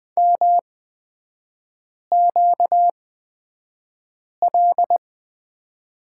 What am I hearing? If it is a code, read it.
Morse code, heard as a keyed tone, about 20 wpm, spaced as for 5 wpm, 700 Hz. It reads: MQL
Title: 236 Lesson 38e - Letter q - Three Characters 20wpm